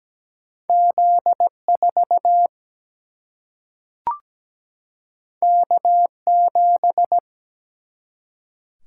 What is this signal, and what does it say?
Morse code, heard as a keyed tone, about 17 wpm, 700 Hz. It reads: Z4 K7
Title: Single-Letter Single-Number 17wpm